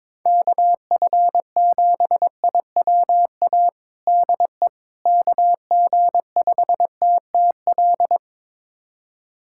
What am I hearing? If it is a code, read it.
Morse code, heard as a keyed tone, about 22 wpm, 700 Hz. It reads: KF7IWA DE KG5TTL
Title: USA Callsign DE Callsign 22wpm